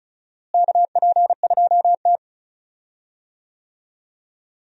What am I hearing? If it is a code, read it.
Morse code, heard as a keyed tone, about 36 wpm, 700 Hz. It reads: KP2T
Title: DXCC-100 Call Signs - 4 Characters - Repeated 3x - 35wpm